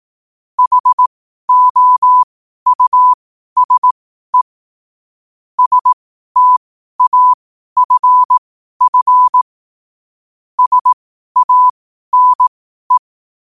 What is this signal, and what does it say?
Morse code, heard as a keyed tone, about 18 wpm, spaced as for 12 wpm, 1000 Hz. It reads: HOUSE STAFF SANE
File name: qotdfiles / 12wpm